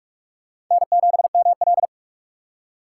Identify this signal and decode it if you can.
Morse code, heard as a keyed tone, about 46 wpm, 700 Hz. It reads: N7ML
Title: K1USN SST QSO Practice - 45wpm